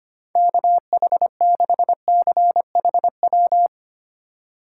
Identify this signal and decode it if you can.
Morse code, heard as a keyed tone, about 25 wpm, 700 Hz. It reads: KH6CHW
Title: USA Call Signs - 6 Characters - 3x Character Spacing - 25wpm